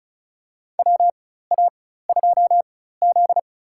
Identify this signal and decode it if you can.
Morse code, heard as a keyed tone, about 36 wpm, spaced as for 16 wpm, 700 Hz. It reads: WA2Z
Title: USA Call Signs - 4x Character Spacing - 35wpm